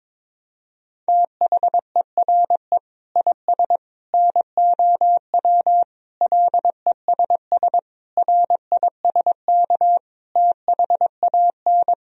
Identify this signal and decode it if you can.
Morse code, heard as a keyed tone, about 22 wpm, 700 Hz. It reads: THERE IS NOW LESS RISK THAN
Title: Sentences from Top 1000 Words - Review Part 4 - 22wpm